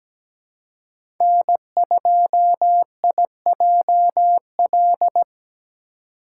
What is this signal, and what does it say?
Morse code, heard as a keyed tone, about 17 wpm, 700 Hz. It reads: N2IJL
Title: USA Call Signs - 5 Characters - 2x Character Spacing - 17wpm